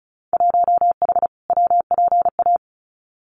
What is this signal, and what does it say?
Morse code, heard as a keyed tone, about 34 wpm, 700 Hz. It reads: 1H WPA
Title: Winter Field Day Exchanges 35wpm